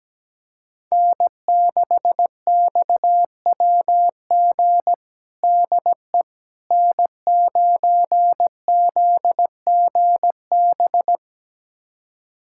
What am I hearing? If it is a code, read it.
Morse code, heard as a keyed tone, about 17 wpm, 700 Hz. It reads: N6XWG DE N9ZGB
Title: USA Callsign DE Callsign - 5 Characters - 17wpm